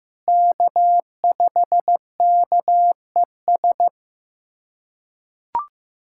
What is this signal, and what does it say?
Morse code, heard as a keyed tone, about 16 wpm, 700 Hz. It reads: K5KES
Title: USA Callsign DE Callsign - 5 Characters - 15wpm